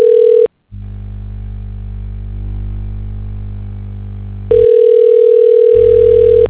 Ringback tone (North America), 440 + 480 Hz.